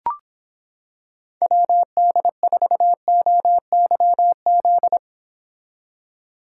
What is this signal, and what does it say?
Morse code, heard as a keyed tone, about 25 wpm, 700 Hz. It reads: WD4OYZ